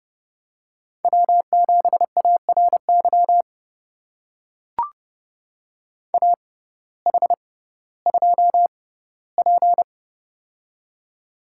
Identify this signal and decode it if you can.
Morse code, heard as a keyed tone, about 32 wpm, spaced as for 17 wpm, 700 Hz. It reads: W7ARY AH2P
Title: USA Call Signs - 6x Character Spacing - 30wpm